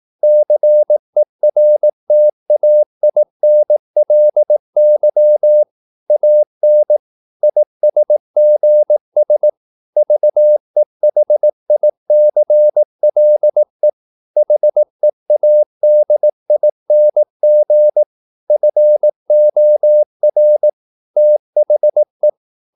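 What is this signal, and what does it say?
Morse code, heard as a keyed tone, about 18 wpm, 600 Hz. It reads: CERTAINLY AN ISGS VEHICLE HEADING FOR THE
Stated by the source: Morse practice at 18 wpm, English text, from CW med Gnister.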